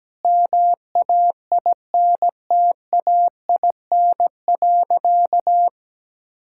Morse code, heard as a keyed tone, about 17 wpm, 700 Hz. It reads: MAINTAIN.
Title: Sentences from Top 1000 Words - Review Part 5 - 17wpm